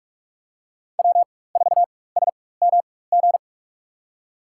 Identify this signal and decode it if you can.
Morse code, heard as a keyed tone, about 46 wpm, spaced as for 21 wpm, 700 Hz. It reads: W4SMG
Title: USA Call Signs - 5 Characters - 4x Character Spacing - 45wpm